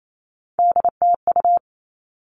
Morse code, heard as a keyed tone, about 28 wpm, 700 Hz. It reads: DTU
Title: All Cut Numbers - 3-Digit 28wpm